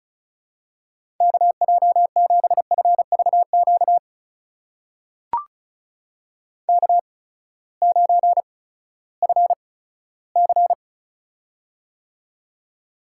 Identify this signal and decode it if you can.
Morse code, heard as a keyed tone, about 36 wpm, spaced as for 18 wpm, 700 Hz. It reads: KJ7FVQ K9FC
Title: USA Call Signs - 8x Character Spacing - 35wpm